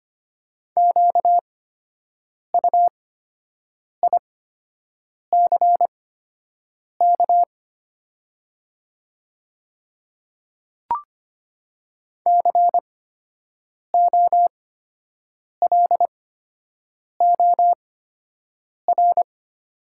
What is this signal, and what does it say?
Morse code, heard as a keyed tone, about 25 wpm, spaced as for 7 wpm, 690 Hz. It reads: QUICK COLOR
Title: 8x Character Spacing - Top 500 Words - 25wpm